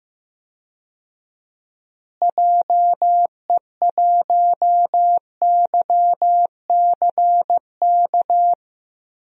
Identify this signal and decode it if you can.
Morse code, heard as a keyed tone, about 16 wpm, 700 Hz. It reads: JE1YCK
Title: DXCC-100 Call Signs - 6 Characters - Repeated 3x - 15wpm